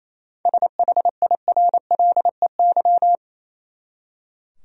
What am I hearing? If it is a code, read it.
Morse code, heard as a keyed tone, about 27 wpm, 700 Hz. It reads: SHIRLEY
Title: USA Common Names - 28wpm